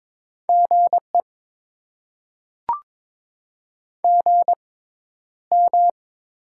Morse code, heard as a keyed tone, about 22 wpm, spaced as for 11 wpm, 690 Hz. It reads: GE GM
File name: Top 20 QSO Elements - 6x Character Spacing - 22wpm